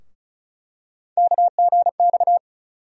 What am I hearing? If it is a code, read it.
Morse code, heard as a keyed tone, about 34 wpm, 700 Hz. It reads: KGX
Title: USA Call Sign Postfixes - 3 Letters 35wpm